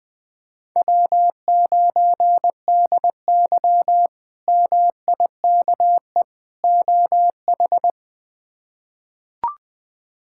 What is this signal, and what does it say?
Morse code, heard as a keyed tone, about 20 wpm, 700 Hz. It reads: W9DY MIKE OH
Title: K1USN SST QSO Practice - 20wpm